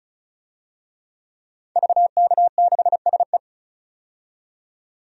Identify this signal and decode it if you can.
Morse code, heard as a keyed tone, about 36 wpm, 700 Hz. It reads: VK6SE
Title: DXCC-100 Call Signs - 5 Characters - 35wpm